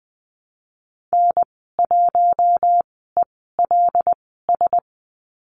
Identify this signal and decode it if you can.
Morse code, heard as a keyed tone, about 20 wpm, spaced as for 15 wpm, 700 Hz. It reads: N1ELS
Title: 082 Lesson 14g - Number 2 - Call Signs 20wpm